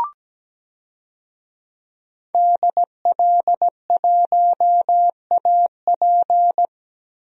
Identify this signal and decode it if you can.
Morse code, heard as a keyed tone, about 17 wpm, 700 Hz. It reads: DL1AP
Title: DXCC-100 Call Signs - 5 Characters - Repeated 2x - 17wpm